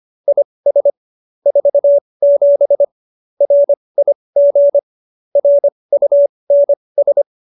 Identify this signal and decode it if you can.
Morse code, heard as a keyed tone, about 25 wpm, spaced as for 20 wpm, 570 Hz. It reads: IS 47 RIG RUNS